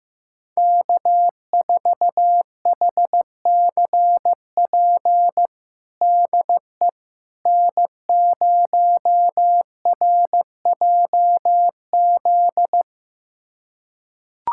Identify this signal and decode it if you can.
Morse code, heard as a keyed tone, about 15 wpm, 700 Hz. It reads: K4HCP DE N0RJZ
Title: USA Callsign DE Callsign - 5 Characters - 15wpm